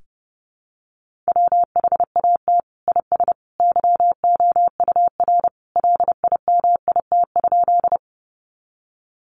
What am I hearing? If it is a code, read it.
Morse code, heard as a keyed tone, about 30 wpm, 700 Hz. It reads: WHAT IS YOUR LIMIT?
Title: Sentences from Top 1500 Words 30wpm